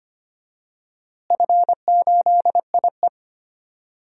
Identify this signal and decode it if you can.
Morse code, heard as a keyed tone, about 25 wpm, 700 Hz. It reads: F8IE
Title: DXCC-100 Call Signs - 4 Characters - Repeated 2x - 25wpm